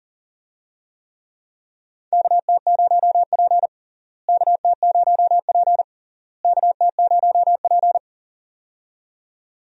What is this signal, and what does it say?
Morse code, heard as a keyed tone, about 40 wpm, 700 Hz. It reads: KT0P KT0P KT0P
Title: USA Call Signs - 4 Characters - Repeated 3x - 40wpm